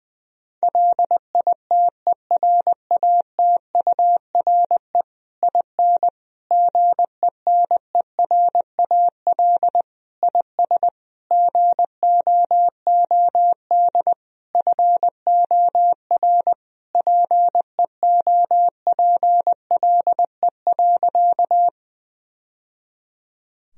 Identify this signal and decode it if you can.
Morse code, heard as a keyed tone, about 21 wpm, 700 Hz. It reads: LITERATURE IN GENERAL IS GOOD FOR PEOPLE.
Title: Sentences from Top 1000 Words - Review Part 3 - 20wpm